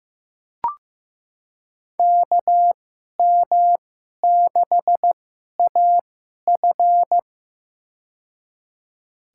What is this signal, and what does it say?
Morse code, heard as a keyed tone, about 15 wpm, spaced as for 11 wpm, 700 Hz. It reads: KM6AF
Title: USA Call Signs - 2x Character Spacing - 15wpm